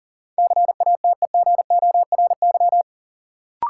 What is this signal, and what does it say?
Morse code, heard as a keyed tone, about 40 wpm, 700 Hz. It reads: CATEGORY
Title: Sentences from Top 1000 Words - Review Part 7 - New Word Drill - 40wpm